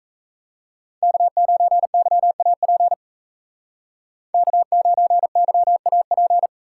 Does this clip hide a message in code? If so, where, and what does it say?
Yes: Morse code from 1.0 to 6.5 s, heard as a keyed tone, about 36 wpm, 700 Hz. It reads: K9YAP K9YAP